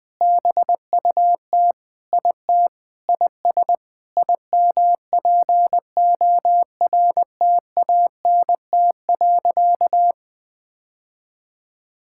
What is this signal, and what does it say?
Morse code, heard as a keyed tone, about 20 wpm, 700 Hz. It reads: BUT IT IS IMPORTANT.
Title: Sentences with 8-Word Phrases - Spoken Beginning - 20wpm